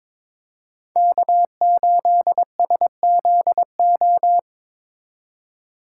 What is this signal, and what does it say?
Morse code, heard as a keyed tone, about 22 wpm, 700 Hz. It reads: K8SZO
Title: USA Call Signs - 5 Characters - 2x Character Spacing - 22wpm